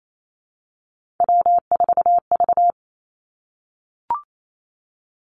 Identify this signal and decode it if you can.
Morse code, heard as a keyed tone, about 28 wpm, 700 Hz. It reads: W4V
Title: Three Challenging Letter-Number Combinations 28wpm